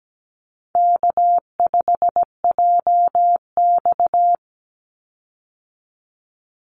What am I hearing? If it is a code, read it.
Morse code, heard as a keyed tone, about 17 wpm, 700 Hz. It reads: K5JX
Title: Call Signs Repeated 2x - 17wpm